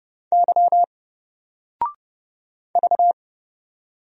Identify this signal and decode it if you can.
Morse code, heard as a keyed tone, about 30 wpm, 700 Hz. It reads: Y V
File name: Single-Letter-Number-30wpm